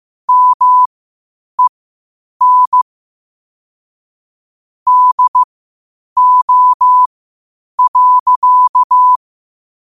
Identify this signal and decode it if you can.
Morse code, heard as a keyed tone, about 15 wpm, spaced as for 8 wpm, 1000 Hz. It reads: MEN DO.
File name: qotdfiles / 8wpm